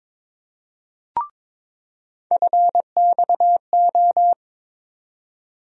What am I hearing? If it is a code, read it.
Morse code, heard as a keyed tone, about 22 wpm, 700 Hz. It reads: FXO
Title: USA Call Sign Postfixes - 3 Letters 22wpm